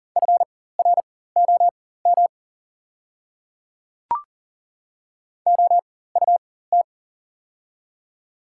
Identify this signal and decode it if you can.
Morse code, heard as a keyed tone, about 40 wpm, spaced as for 19 wpm, 690 Hz. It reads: FROM OUT
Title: 4x Character Spacing - Top 200 Words - 40wpm